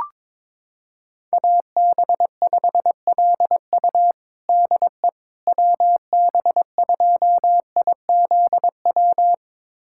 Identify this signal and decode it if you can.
Morse code, heard as a keyed tone, about 22 wpm, 700 Hz. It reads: AB5LU DE WB2IZW
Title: USA Callsign DE Callsign 22wpm